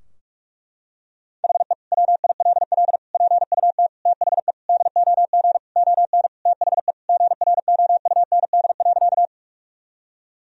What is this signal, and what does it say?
Morse code, heard as a keyed tone, about 46 wpm, 700 Hz. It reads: HE WILL PUT THE DOG ON THE GROUND.
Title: Sentences from Top 1000 Words - Review Part 2 - 45wpm